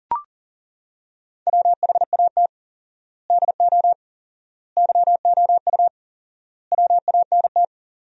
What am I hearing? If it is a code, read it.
Morse code, heard as a keyed tone, about 37 wpm, 700 Hz. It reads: WHAT DO YOU WANT
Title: 4x Word-Spacing - Sentences from Top 200 Words - 40wpm